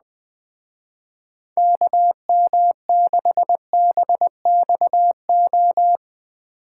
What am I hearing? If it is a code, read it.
Morse code, heard as a keyed tone, about 21 wpm, 700 Hz. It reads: KM6BXO